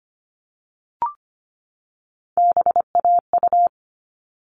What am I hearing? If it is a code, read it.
Morse code, heard as a keyed tone, about 25 wpm, 700 Hz. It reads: BAU
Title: All Cut Numbers - 3-Digit 25wpm